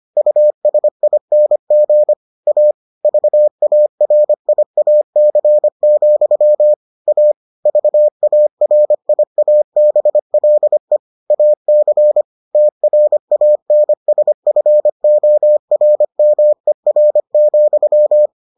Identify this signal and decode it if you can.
Morse code, heard as a keyed tone, about 25 wpm, 600 Hz. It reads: USING A VARIAC, A VARIABLE AC TRANSFORMER,